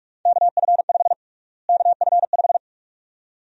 Morse code, heard as a keyed tone, about 46 wpm, 700 Hz. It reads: KF5 KF5
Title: USA Call Sign Prefixes - Repeated 2x - 45wpm